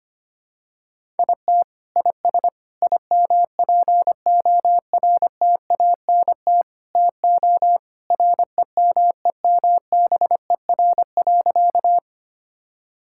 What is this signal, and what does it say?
Morse code, heard as a keyed tone, about 25 wpm, 700 Hz. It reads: IT IS IMPORTANT TO REMEMBER.
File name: practice / 25wpm / Sentences with 4-Word Phrases - 25wpm